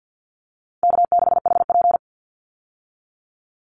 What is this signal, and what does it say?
Morse code, heard as a keyed tone, about 50 wpm, 700 Hz. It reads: K6HP
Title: Easy Call Signs Repeated 2x - 50wpm